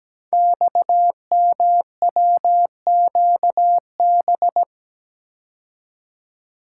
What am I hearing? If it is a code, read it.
Morse code, heard as a keyed tone, about 17 wpm, 700 Hz. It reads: XMWQB
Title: Code Groups - 5 Letter:Numbers - 17wpm